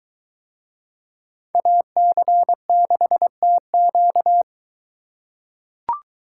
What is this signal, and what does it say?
Morse code, heard as a keyed tone, about 23 wpm, 700 Hz. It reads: AC6TQ